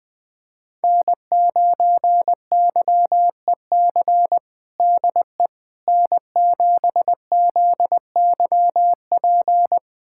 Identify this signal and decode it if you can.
Morse code, heard as a keyed tone, about 21 wpm, 700 Hz. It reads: N9YEC DE N7ZYP
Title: USA Callsign DE Callsign - 5 Characters - 20wpm